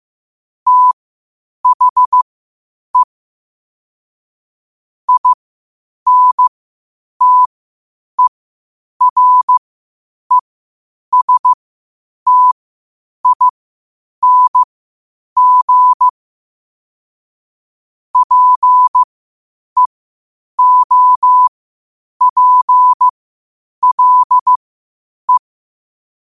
Morse code, heard as a keyed tone, about 15 wpm, spaced as for 8 wpm, 1000 Hz. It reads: THE INTERESTING PEOPLE
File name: qotdfiles / 8wpm